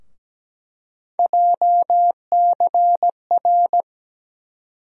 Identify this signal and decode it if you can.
Morse code, heard as a keyed tone, about 17 wpm, 700 Hz. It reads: JCR
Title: USA Call Sign Postfixes - 3 Letters 17wpm